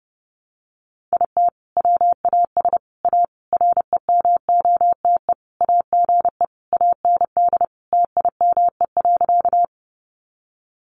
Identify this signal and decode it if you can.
Morse code, heard as a keyed tone, about 30 wpm, 700 Hz. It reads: IT WAS A REMOTE AGE AND TIME.
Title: Sentences from Top 1300 Words 30wpm